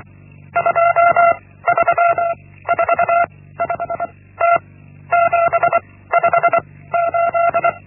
Morse code, heard as a keyed tone, about 24 wpm, 680 Hz. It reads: 2 3 4 5 T 7 5 8
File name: Linksmakalnis-1